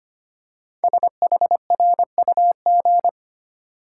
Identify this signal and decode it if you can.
Morse code, heard as a keyed tone, about 25 wpm, 700 Hz. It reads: SHRUG
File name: 5 Letter Words - First 3 Letters Given - 25wpm